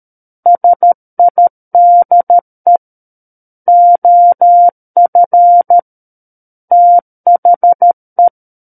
Morse code, heard as a keyed tone, about 13 wpm, 700 Hz. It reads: SIDE OF THE
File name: Book One / 04-Chapter 4 - 13 WPM